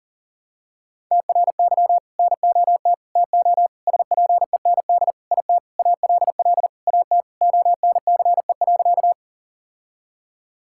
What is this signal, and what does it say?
Morse code, heard as a keyed tone, about 40 wpm, 700 Hz. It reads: TRY NOT TO SPEND IT ALL AT ONCE.
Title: Sentences from Top 1000 Words - Review Part 5 - 40wpm